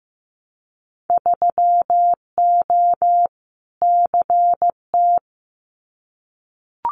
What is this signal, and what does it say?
Morse code, heard as a keyed tone, about 15 wpm, 700 Hz. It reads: 3O CT
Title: Winter Field Day Exchanges Encore 15wpm